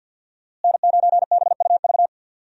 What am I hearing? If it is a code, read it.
Morse code, heard as a keyed tone, about 52 wpm, 700 Hz. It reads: N9BUV
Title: USA Call Signs - 5 Characters - Repeated 3x - 50wpm